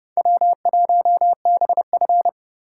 Morse code, heard as a keyed tone, about 29 wpm, 700 Hz. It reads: W1BF